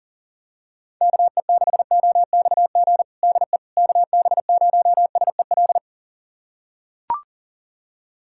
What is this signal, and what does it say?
Morse code, heard as a keyed tone, about 40 wpm, 700 Hz. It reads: KE6OXG DE KB0SEL
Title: USA Callsign DE Callsign - 6 Characters - 40wpm